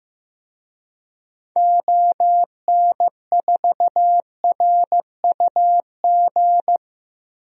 Morse code, heard as a keyed tone, about 15 wpm, 700 Hz. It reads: ON4RUG
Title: DXCC-100 Call Signs - 6 Characters - 15wpm